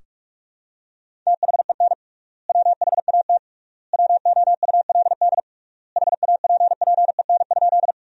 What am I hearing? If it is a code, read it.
Morse code, heard as a keyed tone, about 41 wpm, 700 Hz. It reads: THEN WHAT WOULD HAPPEN?
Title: 3x Word-Spacing - Sentences from Top 400 Words - 45wpm